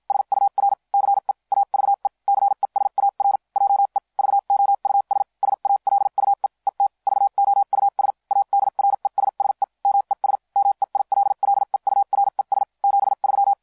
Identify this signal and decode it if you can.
Morse code, heard as a keyed tone, about 56 wpm, spaced as for 45 wpm, 800 Hz. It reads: SUR CE AVE CESAR JE VOUS SALUE ET VOUS ADRESSE MES MEILLEURES 73